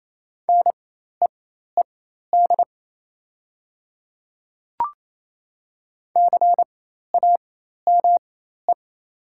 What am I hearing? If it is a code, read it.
Morse code, heard as a keyed tone, about 28 wpm, spaced as for 13 wpm, 700 Hz. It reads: NEED CAME
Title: 4x Character Spacing - Top 300 Words - 28wpm